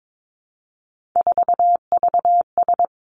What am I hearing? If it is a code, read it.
Morse code, heard as a keyed tone, about 22 wpm, 700 Hz. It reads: 4VS
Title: Three Challenging Letter-Number Combinations 22wpm